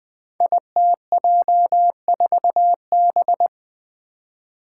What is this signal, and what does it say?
Morse code, heard as a keyed tone, about 20 wpm, 700 Hz. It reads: ITJ4B
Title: Five Challenging Letter-Number Combinations 20wpm